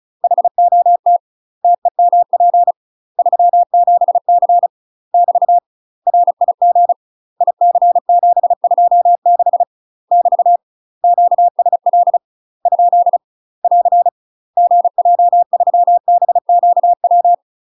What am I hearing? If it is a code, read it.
Morse code, heard as a keyed tone, about 34 wpm, 700 Hz. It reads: HOT TEMP 37C <BT> RIG IC726 <BT> QSL ? <AR> GJ3BQW
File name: UK MASTER QSOs TRAINING / 35WPM / UK MASTER QSOs 0015